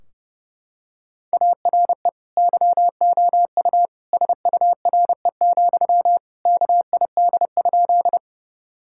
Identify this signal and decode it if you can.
Morse code, heard as a keyed tone, about 30 wpm, 700 Hz. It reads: ARE YOU SURE, KID?
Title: Sentences from Top 1600 Words 30wpm